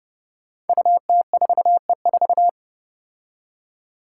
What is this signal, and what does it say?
Morse code, heard as a keyed tone, about 30 wpm, 700 Hz. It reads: UT4E4
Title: Five Challenging Letter-Number Combinations 30wpm